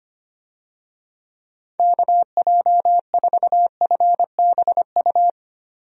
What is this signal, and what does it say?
Morse code, heard as a keyed tone, about 25 wpm, 700 Hz. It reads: KJ4FBU
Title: USA Call Signs - 6 Characters - Repeated 2x - 25wpm